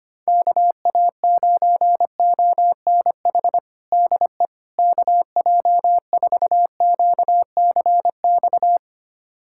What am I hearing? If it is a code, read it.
Morse code, heard as a keyed tone, about 25 wpm, 700 Hz. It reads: KA9ONH DE KJ4QCX
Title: USA Callsign DE Callsign - 6 Characters - 25wpm